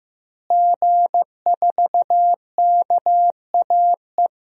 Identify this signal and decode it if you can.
Morse code, heard as a keyed tone, about 15 wpm, 700 Hz. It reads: G4KAE
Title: DXCC-100 Call Signs - 5 Characters - Repeated 2x - 15wpm